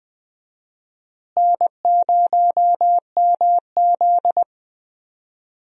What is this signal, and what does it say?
Morse code, heard as a keyed tone, about 20 wpm, 700 Hz. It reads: N0MZ
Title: USA Call Signs - 4 Characters - Repeated 2x - 20wpm